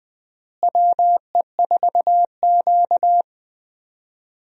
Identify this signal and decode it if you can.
Morse code, heard as a keyed tone, about 20 wpm, 700 Hz. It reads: WE4Q
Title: USA Call Signs - 4 Characters - Repeated 3x - 20wpm